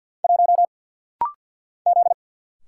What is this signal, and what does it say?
Morse code, heard as a keyed tone, about 50 wpm, 700 Hz. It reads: 1 Z
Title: Single Letter-Number Alt - Rapid-Fire 50wpm